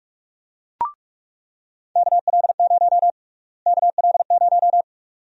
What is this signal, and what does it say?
Morse code, heard as a keyed tone, about 46 wpm, 700 Hz. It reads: KL0 KL0
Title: USA Call Sign Prefixes - Repeated 2x - 45wpm